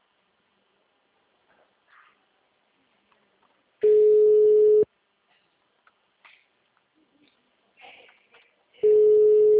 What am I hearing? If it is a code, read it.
Ringback tone (Europe, 425 Hz).